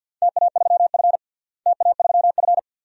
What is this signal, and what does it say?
Morse code, heard as a keyed tone, about 51 wpm, 700 Hz. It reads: TA3F TA3F
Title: DXCC-100 Call Signs - 4 Characters - Repeated 2x - 50wpm